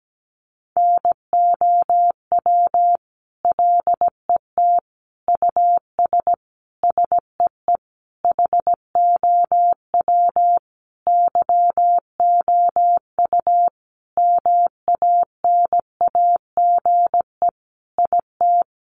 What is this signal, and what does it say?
Morse code, heard as a keyed tone, about 17 wpm, 700 Hz. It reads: NOW LET US SEE HOW YOU MANAGE IT
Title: Sentences from Top 1400 Words 17wpm